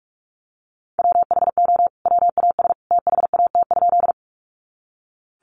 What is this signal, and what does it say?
Morse code, heard as a keyed tone, about 44 wpm, 700 Hz. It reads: WHO WAS THAT?
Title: Sentences from Top 100 Words 45wpm